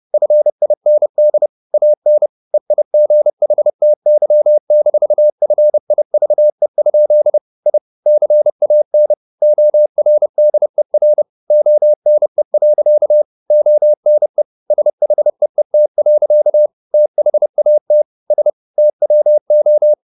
Morse code, heard as a keyed tone, about 30 wpm, 600 Hz. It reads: FIND AN EIGHTY-FIVE? I CAN ORDER ONE. ONE SHEET. THAT S TWO
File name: the old man and the sea / the old man and the sea 1 30wpm